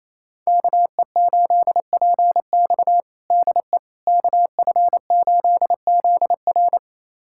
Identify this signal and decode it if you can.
Morse code, heard as a keyed tone, about 28 wpm, 700 Hz. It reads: KE8PX DE KF8ZR
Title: USA Callsign DE Callsign - 5 Characters - 28wpm